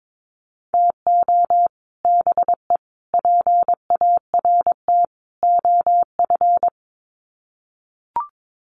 Morse code, heard as a keyed tone, about 22 wpm, 700 Hz. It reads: TO BE PART OF
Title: Sets of 4 Words 22wpm